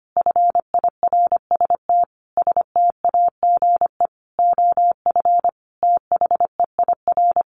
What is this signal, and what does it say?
Morse code, heard as a keyed tone, about 25 wpm, 700 Hz. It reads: FIRST STAGE OF THEIR
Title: Sentences from Top 1200 Words 25wpm